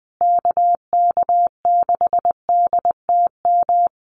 Morse code, heard as a keyed tone, about 20 wpm, 700 Hz. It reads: KK6DTM
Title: USA Call Signs Repeated 3x 20wpm